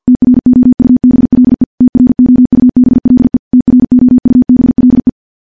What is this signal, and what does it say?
Morse code, heard as a keyed tone, about 50 wpm, 250 Hz. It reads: TROUBLE TROUBLE TROUBLE
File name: morse / morse-training-50WPM